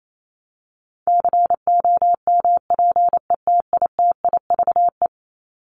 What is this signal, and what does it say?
Morse code, heard as a keyed tone, about 28 wpm, 700 Hz. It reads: COMPETITIVE
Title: Top 1001-1250 Words 28wpm